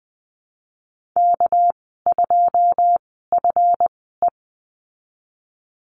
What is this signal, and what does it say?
Morse code, heard as a keyed tone, about 19 wpm, spaced as for 14 wpm, 700 Hz. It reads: K2FE
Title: 200 Lesson 32g - Letter k - Call Signs 20wpm